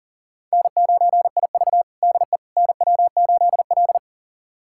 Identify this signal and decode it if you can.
Morse code, heard as a keyed tone, about 40 wpm, 700 Hz. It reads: N9IV DE NW8L
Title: USA Callsign DE Callsign - 4 Characters - 40wpm